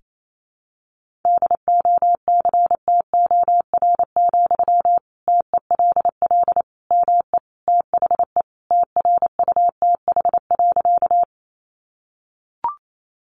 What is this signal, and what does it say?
Morse code, heard as a keyed tone, about 28 wpm, 700 Hz. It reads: DOCTOR, TELL ME THE TRUTH.
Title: Sentences from Top 1500 Words 28wpm